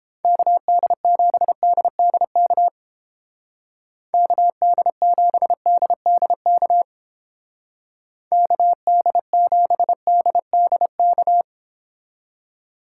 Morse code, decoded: KD7DDK KD7DDK KD7DDK

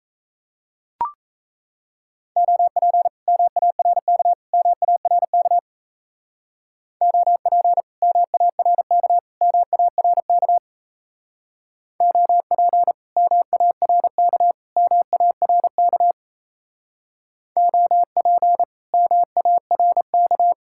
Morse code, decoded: OP MARK MARK OP MARK MARK OP MARK MARK OP MARK

